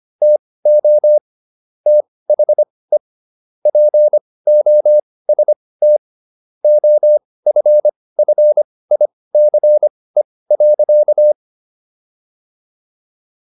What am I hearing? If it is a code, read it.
Morse code, heard as a keyed tone, about 25 wpm, spaced as for 18 wpm, 600 Hz. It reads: TO THE POST OFFICE.